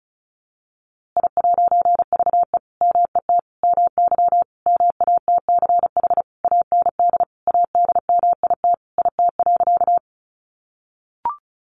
Morse code, heard as a keyed tone, about 34 wpm, 700 Hz. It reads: I'VE MET MY MATCH AND ADMIT IT.
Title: A Sound Of Thunder 35wpm